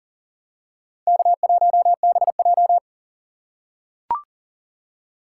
Morse code, heard as a keyed tone, about 40 wpm, 700 Hz. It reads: K1BJ
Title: USA Call Signs - 4 Characters - 4x Character Spacing - 40wpm